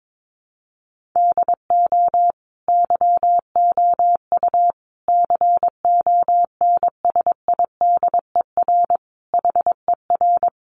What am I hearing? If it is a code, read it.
Morse code, heard as a keyed tone, about 22 wpm, 700 Hz. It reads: DO YOU CONSIDER HER
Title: Sentences from Top 600 Words 22wpm